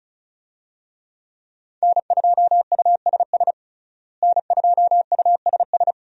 Morse code, heard as a keyed tone, about 36 wpm, 700 Hz. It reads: N2USS N2USS
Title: USA Call Signs - 5 Characters - Repeated 2x - 35wpm